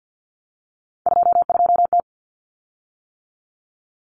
Morse code, heard as a keyed tone, about 50 wpm, 700 Hz. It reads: 22T
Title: Common Cut Numbers - 3-Digit 50wpm